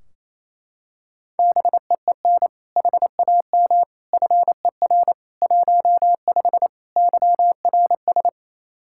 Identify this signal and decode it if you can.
Morse code, heard as a keyed tone, about 27 wpm, 700 Hz. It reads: BEEN HAM FER 15 YRS
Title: K1USN SST QSO Practice 28wpm